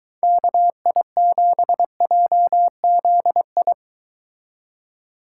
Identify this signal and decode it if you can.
Morse code, heard as a keyed tone, about 22 wpm, 700 Hz. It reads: KI7JZI